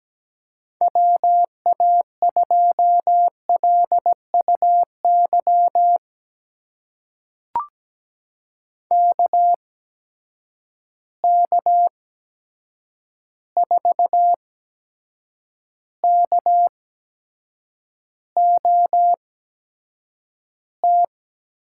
Morse code, heard as a keyed tone, about 17 wpm, spaced as for 7 wpm, 700 Hz. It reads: WA2LUY KK4KOT
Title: USA Call Signs - 6 Characters - 8x Character Spacing - 17wpm